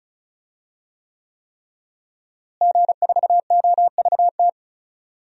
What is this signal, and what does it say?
Morse code, heard as a keyed tone, about 34 wpm, 700 Hz. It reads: G4OVT
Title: DXCC-100 Call Signs - 5 Characters - Repeated 3x - 35wpm